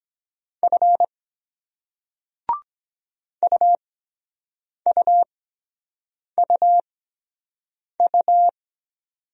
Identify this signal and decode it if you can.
Morse code, heard as a keyed tone, about 23 wpm, spaced as for 7 wpm, 700 Hz. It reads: F UUUU